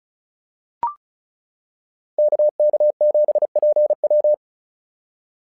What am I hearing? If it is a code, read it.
Morse code, heard as a keyed tone, about 36 wpm, 600 Hz. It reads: KK7PW
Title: SOTA QSO Practice 35wpm